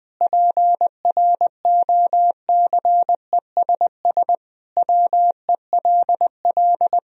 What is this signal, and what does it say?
Morse code, heard as a keyed tone, about 20 wpm, 700 Hz. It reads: PROCESS WELL
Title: Sentences from Top 1000 Words - Review Part 3 - 20wpm